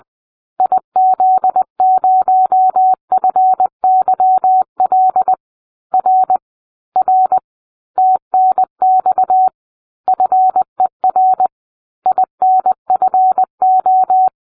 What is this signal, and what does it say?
Morse code, heard as a keyed tone, about 20 wpm, 750 Hz. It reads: IZ0FYL R R TNX FER INFO